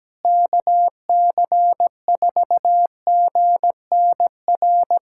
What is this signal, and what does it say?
Morse code, heard as a keyed tone, about 17 wpm, 700 Hz. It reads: KC4GNR
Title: USA Callsign DE Callsign - 6 Characters - 17wpm